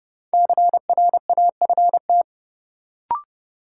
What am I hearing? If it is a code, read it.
Morse code, heard as a keyed tone, about 30 wpm, 700 Hz. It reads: CRAFT
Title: Sentences from Top 1000 Words - Review Part 6 - New Word Drill - 30wpm